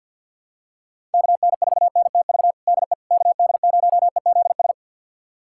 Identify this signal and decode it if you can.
Morse code, heard as a keyed tone, about 52 wpm, 700 Hz. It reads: KN4NTV DE KD0EZS
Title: USA Callsign DE Callsign 50wpm